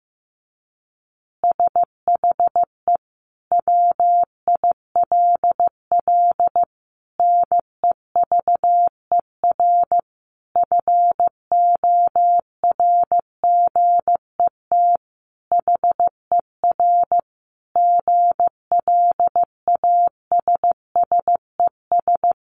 Morse code, heard as a keyed tone, about 15 wpm, 700 Hz. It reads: SHE WILL NEVER FORGET HER GLASSES
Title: Additional Easy Phrases 15wpm